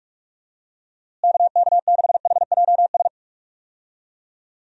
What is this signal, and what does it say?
Morse code, heard as a keyed tone, about 41 wpm, 700 Hz. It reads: KK6HJS
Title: USA Call Signs - 6 Characters - Repeated 2x - 45wpm